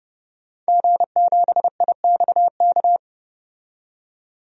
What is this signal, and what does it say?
Morse code, heard as a keyed tone, about 30 wpm, 700 Hz. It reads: G7IXK
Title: DXCC-100 Call Signs - 5 Characters - 30wpm